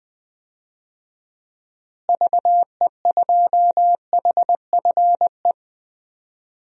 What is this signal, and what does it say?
Morse code, heard as a keyed tone, about 20 wpm, 700 Hz. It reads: VE2HFE
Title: DXCC-100 Call Signs - 6 Characters - Repeated 2x - 20wpm